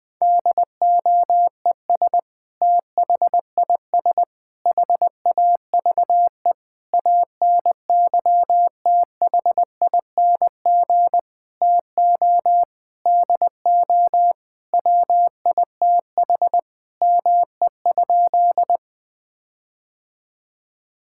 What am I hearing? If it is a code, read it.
Morse code, heard as a keyed tone, about 20 wpm, 700 Hz. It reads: DOES THIS HAVE ANYTHING TO DO WITH ME?
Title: Sentences with 5-Word Phrases - Spoken Beginning - 20wpm